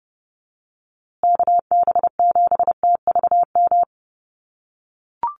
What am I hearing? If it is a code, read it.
Morse code, heard as a keyed tone, about 30 wpm, 700 Hz. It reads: KB7TVM
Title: USA Call Signs Repeated 3x 30wpm